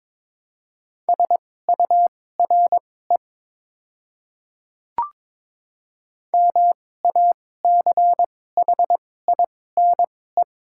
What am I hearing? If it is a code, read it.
Morse code, heard as a keyed tone, about 22 wpm, spaced as for 16 wpm, 700 Hz. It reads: SURE MACHINE
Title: 2x Character Spacing - Top 500 Words - 22wpm